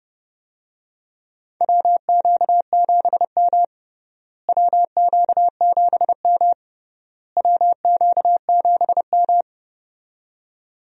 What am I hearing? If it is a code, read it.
Morse code, heard as a keyed tone, about 30 wpm, 700 Hz. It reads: WQ7M WQ7M WQ7M
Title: USA Call Signs - 4 Characters - Repeated 3x - 30wpm